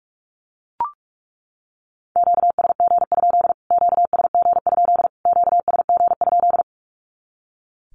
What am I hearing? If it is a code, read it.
Morse code, heard as a keyed tone, about 44 wpm, 700 Hz. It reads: QSG? QSG? QSG?
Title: Q-Codes Repeated 3x 45wpm